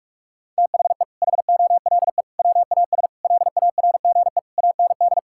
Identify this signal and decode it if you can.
Morse code, heard as a keyed tone, about 46 wpm, 700 Hz. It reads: THE HOLE WAS LARGE AND
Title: Sentences from Top 1600 Words 45wpm